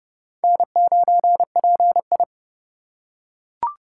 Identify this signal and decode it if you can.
Morse code, heard as a keyed tone, about 30 wpm, 700 Hz. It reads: N9PI
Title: USA Call Signs - 3x Character Spacing - 30wpm